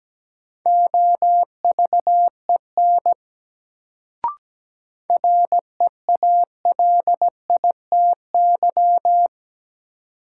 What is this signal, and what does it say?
Morse code, heard as a keyed tone, about 17 wpm, 700 Hz. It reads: OVEN REALITY
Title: Sentences from Top 1200 Words - New Word Drill Alt - 17wpm